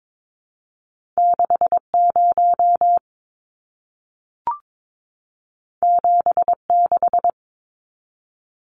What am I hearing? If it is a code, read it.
Morse code, heard as a keyed tone, about 22 wpm, 700 Hz. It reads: 60 76
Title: Numbers - 2-Digit 22wpm